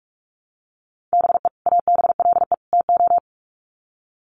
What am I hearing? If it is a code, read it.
Morse code, heard as a keyed tone, about 44 wpm, 700 Hz. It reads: BE ABLE TO
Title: Sets of 3 Words 45wpm